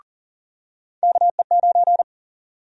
Morse code, heard as a keyed tone, about 40 wpm, 700 Hz. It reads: KE9
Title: USA Call Sign Prefixes - 40wpm